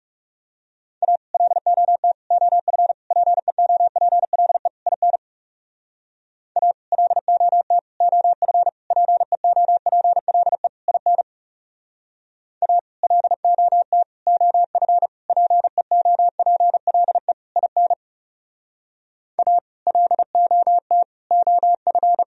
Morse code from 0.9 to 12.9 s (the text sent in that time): A LOT OF PEOPLE IN A LOT OF PEOPLE IN A